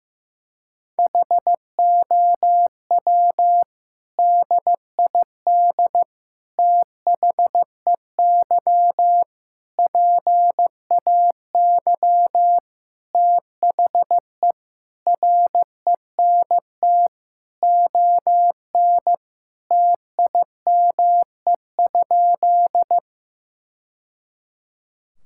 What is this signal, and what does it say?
Morse code, heard as a keyed tone, about 15 wpm, 700 Hz. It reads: HOW DID THEY PAY THE RENT ON TIME?
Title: Sentences from Top 1000 Words - Review Part 5 - 15wpm